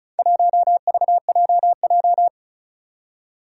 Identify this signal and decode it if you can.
Morse code, heard as a keyed tone, about 36 wpm, 700 Hz. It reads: 1VJJ
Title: Four Challenging Letter-Number Combinations 35wpm